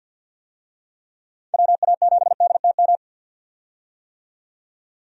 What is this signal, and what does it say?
Morse code, heard as a keyed tone, about 53 wpm, 700 Hz. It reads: WA7DTM
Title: USA Call Signs - 6 Characters - 50wpm